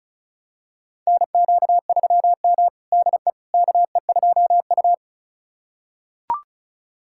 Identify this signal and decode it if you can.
Morse code, heard as a keyed tone, about 34 wpm, 700 Hz. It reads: NQ3M DE KE2U
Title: USA Callsign DE Callsign - 4 Characters - 35wpm